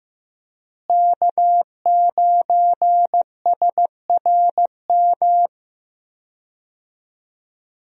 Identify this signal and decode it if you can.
Morse code, heard as a keyed tone, about 15 wpm, 700 Hz. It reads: K9SRM
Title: USA Call Signs - 5 Characters - Repeated 2x - 15wpm